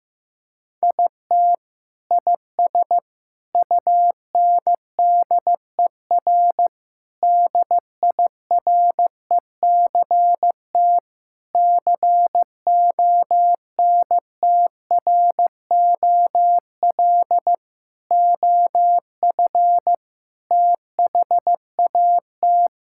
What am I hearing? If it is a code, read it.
Morse code, heard as a keyed tone, about 15 wpm, 700 Hz. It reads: IT IS UNDER DIRECT CONTROL OF THAT
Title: Sentences from Top 1000 Words - Review Part 5 - 15wpm